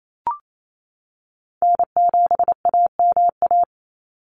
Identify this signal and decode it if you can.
Morse code, heard as a keyed tone, about 28 wpm, 700 Hz. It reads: N7AMA
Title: NA SOTA Chaser Callsigns 28wpm